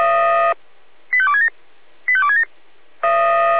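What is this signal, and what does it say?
Five-tone selective call (ZVEI), heard as quick runs of five tones, tones of about 70 ms. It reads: 85267 85267
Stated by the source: ZVEI five-tone call 85270 received by SDR, archive path 496.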